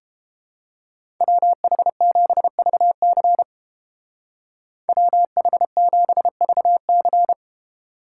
Morse code, heard as a keyed tone, about 29 wpm, 700 Hz. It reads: WH7VC WH7VC